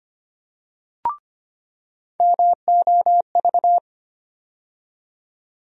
Morse code, heard as a keyed tone, about 25 wpm, 700 Hz. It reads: MOV
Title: USA Call Sign Postfixes - 3 Letters 25wpm